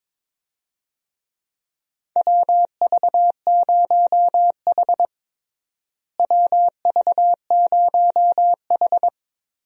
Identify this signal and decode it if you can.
Morse code, heard as a keyed tone, about 22 wpm, 700 Hz. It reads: WV0H WV0H
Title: USA Call Signs - 4 Characters - Repeated 3x - 22wpm